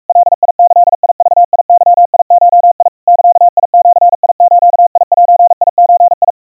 Morse code (Japanese Wabun code), heard as a keyed tone, about 44 wpm, 700 Hz. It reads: ガギグゲゴ ザジズゼゾ
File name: 150 cpm 700 Hz Crispy